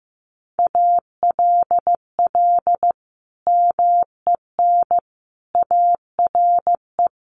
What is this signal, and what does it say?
Morse code, heard as a keyed tone, about 15 wpm, 700 Hz. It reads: ALL MEN ARE
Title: Sentences from Top 1500 Words 15wpm